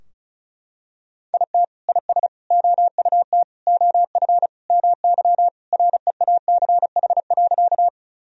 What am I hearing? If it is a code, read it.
Morse code, heard as a keyed tone, about 34 wpm, 700 Hz. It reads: IT IS OUT OF MY REACH.
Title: Sentences from Top 1000 Words - Review Part 2 - 35wpm